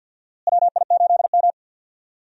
Morse code, heard as a keyed tone, about 52 wpm, 700 Hz. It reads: WI8M
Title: USA Call Signs - 4 Characters - Repeated 2x - 50wpm